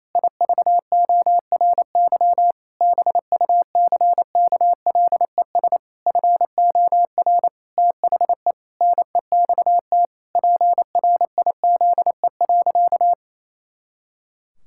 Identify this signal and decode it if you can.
Morse code, heard as a keyed tone, about 28 wpm, 700 Hz. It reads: IVORY BUCKLES FOR THE NEXT PRIZE.
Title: Pangram Challenge v2 - 28wpm